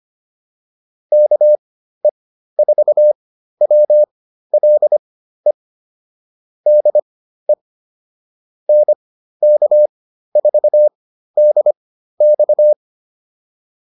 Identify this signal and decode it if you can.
Morse code, heard as a keyed tone, about 25 wpm, spaced as for 13 wpm, 600 Hz. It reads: KE4WLE DE NK4DX